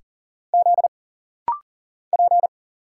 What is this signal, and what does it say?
Morse code, heard as a keyed tone, about 40 wpm, 700 Hz. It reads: Z P
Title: Single Letters - Rapid-Fire 40wpm